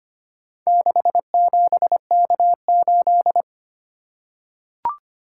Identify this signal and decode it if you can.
Morse code, heard as a keyed tone, about 25 wpm, 700 Hz. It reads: 67K8
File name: practice / 25wpm / Four Letter-Number Combinations - 25wpm